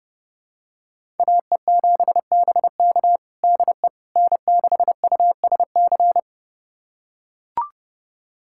Morse code, heard as a keyed tone, about 30 wpm, 700 Hz. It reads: AE7BK DE N6USC
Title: USA Callsign DE Callsign - 5 Characters - 30wpm